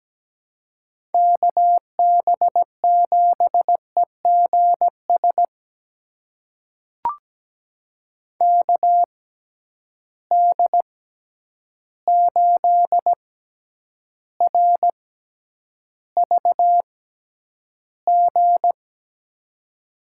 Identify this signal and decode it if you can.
Morse code, heard as a keyed tone, about 17 wpm, spaced as for 9 wpm, 700 Hz. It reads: KB7EGS KD8RVG
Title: USA Call Signs - 6 Characters - 6x Character Spacing - 17wpm